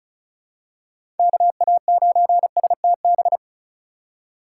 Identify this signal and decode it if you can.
Morse code, heard as a keyed tone, about 34 wpm, 700 Hz. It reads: KA9STB
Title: USA Call Signs - 6 Characters - 2x Character Spacing - 35wpm